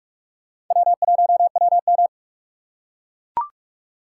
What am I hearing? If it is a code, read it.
Morse code, heard as a keyed tone, about 43 wpm, 700 Hz. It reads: W1WM
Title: USA Call Signs - 4 Characters - 2x Character Spacing - 45wpm